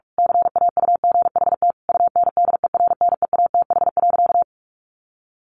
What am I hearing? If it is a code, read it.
Morse code, heard as a keyed tone, about 44 wpm, 700 Hz. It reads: CAUGHT UNDERNEATH.
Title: A Sound Of Thunder 45wpm